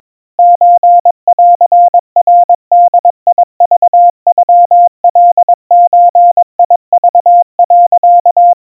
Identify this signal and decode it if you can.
Morse code (Japanese Wabun code), heard as a keyed tone, about 22 wpm, 700 Hz. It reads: ソンナボクノカゾク、
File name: 80 cpm 700 Hz Crispy